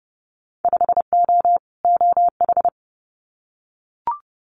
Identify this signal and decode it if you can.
Morse code, heard as a keyed tone, about 30 wpm, 700 Hz. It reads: 5O OH
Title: Winter Field Day Exchanges Encore 30wpm